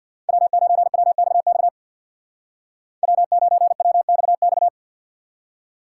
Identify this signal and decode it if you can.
Morse code, heard as a keyed tone, about 60 wpm, 700 Hz. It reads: W9WXX W9WXX
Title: Speed Racing - USA 1x3 Call Signs 50wpm